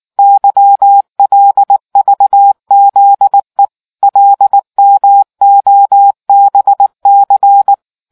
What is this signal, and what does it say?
Morse code, heard as a keyed tone, about 19 wpm, 800 Hz. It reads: YLVZE LMOBC